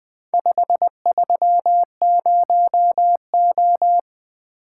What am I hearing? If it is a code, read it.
Morse code, heard as a keyed tone, about 20 wpm, 700 Hz. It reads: 530O
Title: Four Letter-Number Combinations - 20wpm